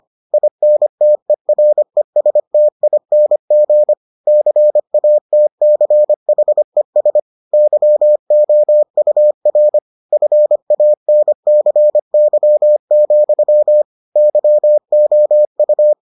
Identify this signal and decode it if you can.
Morse code, heard as a keyed tone, about 25 wpm, 600 Hz. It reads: INTERESTING CATCHES YOUR FANCY, YOU